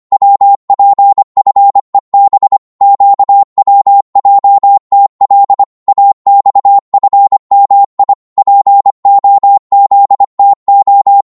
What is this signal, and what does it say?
Morse code, heard as a keyed tone, about 25 wpm, 800 Hz. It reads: WPFEB QWJTL AXFMI POZTO
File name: Morse / ltr-25wpm